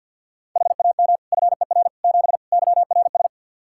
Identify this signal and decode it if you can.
Morse code, heard as a keyed tone, about 52 wpm, 700 Hz. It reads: HAM FER 7 YRS
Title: Practice QSO - Exchange 3 - More Info - 50wpm